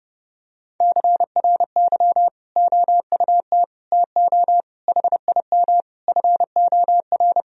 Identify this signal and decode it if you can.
Morse code, heard as a keyed tone, about 30 wpm, 700 Hz. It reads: CRY OUT TO HIM FOR
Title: Sentences from Top 1000 Words - Review Part 2 - 30wpm